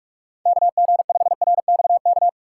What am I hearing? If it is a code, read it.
Morse code, heard as a keyed tone, about 44 wpm, 700 Hz. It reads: KG5RXK
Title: USA Call Signs - 6x Character Spacing - 45wpm